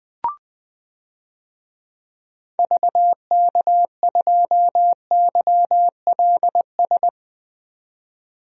Morse code, heard as a keyed tone, about 20 wpm, 700 Hz. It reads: VK2YLS
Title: DXCC-100 Call Signs - 6 Characters - 20wpm